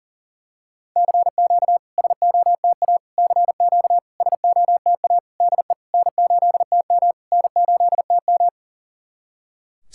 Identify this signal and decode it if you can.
Morse code, heard as a keyed tone, about 40 wpm, 700 Hz. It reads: CQ SOTA CQ SOTA DE N8TM N8TM
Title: SOTA QSO Practice 40wpm